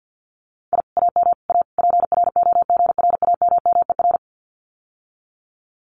Morse code, heard as a keyed tone, about 50 wpm, 700 Hz. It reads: I AM A PROGRAMMER
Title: Common QSO Phrases Encore 50wpm